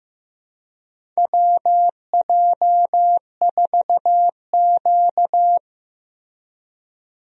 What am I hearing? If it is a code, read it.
Morse code, heard as a keyed tone, about 15 wpm, 700 Hz. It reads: WJ4Q
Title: USA Call Signs - 4 Characters - Repeated 3x - 15wpm